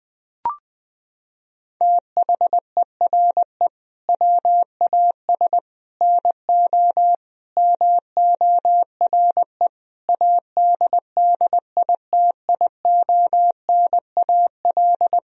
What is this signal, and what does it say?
Morse code, heard as a keyed tone, about 20 wpm, 700 Hz. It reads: THERE WAS NO MORE ADDITIONAL
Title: Sentences from Top 1000 Words - Review Part 4 - 20wpm